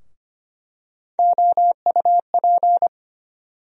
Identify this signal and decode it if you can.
Morse code, heard as a keyed tone, about 25 wpm, 700 Hz. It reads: OUP
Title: USA Call Sign Postfixes - 3 Letters 25wpm